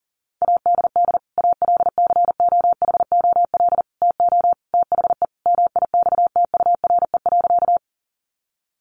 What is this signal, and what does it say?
Morse code, heard as a keyed tone, about 40 wpm, 700 Hz. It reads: ADD ALCOHOL TO THE MIXTURE.
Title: Sentences from Top 1500 Words 40wpm